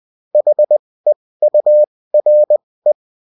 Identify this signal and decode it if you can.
Morse code, heard as a keyed tone, about 20 wpm, spaced as for 16 wpm, 600 Hz. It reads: HEURE
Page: bulletin-uft-sous-forme-texte-et-en-cw